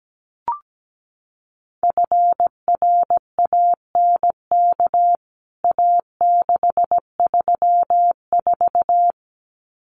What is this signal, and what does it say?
Morse code, heard as a keyed tone, about 17 wpm, 700 Hz. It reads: FRANK A634
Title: CWT Exchange 17wpm